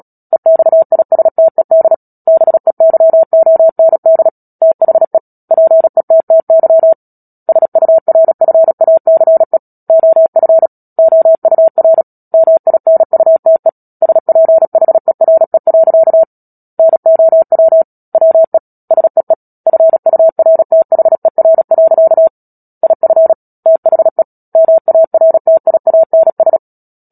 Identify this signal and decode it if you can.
Morse code, heard as a keyed tone, about 37 wpm, 650 Hz. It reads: EXISTED BEYOND THE PETTY SURFACE OF OUR MINUTE SPHERE. NOW WE SEE FURTHER. IF THE MARTIANS
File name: Book Two / 10-Chapter 10 - Epilogue - 36 WPM